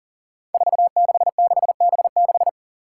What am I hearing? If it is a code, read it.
Morse code, heard as a keyed tone, about 40 wpm, 700 Hz. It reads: 466B6
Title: Five Challenging Letter-Number Combinations 40wpm